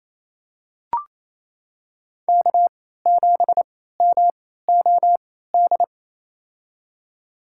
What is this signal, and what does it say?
Morse code, heard as a keyed tone, about 28 wpm, spaced as for 16 wpm, 700 Hz. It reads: K7MOD
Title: USA Call Signs - 3x Character Spacing - 28wpm